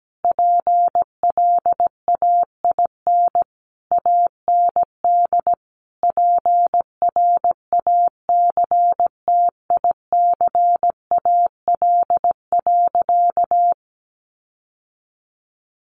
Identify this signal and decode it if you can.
Morse code, heard as a keyed tone, about 17 wpm, 700 Hz. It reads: PLAIN AND PRACTICAL.
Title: Sentences from Top 1400 Words 17wpm